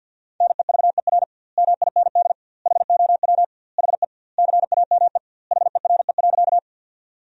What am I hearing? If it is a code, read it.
Morse code, heard as a keyed tone, about 50 wpm, 700 Hz. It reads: NEVER MIND HOW HE CAME HERE.
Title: 2x Word-Spacing - Sentences from Top 500 Words - 50wpm